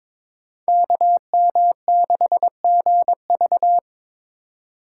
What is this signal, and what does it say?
Morse code, heard as a keyed tone, about 22 wpm, 700 Hz. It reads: KM6GV
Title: USA Call Signs - 2x Character Spacing - 22wpm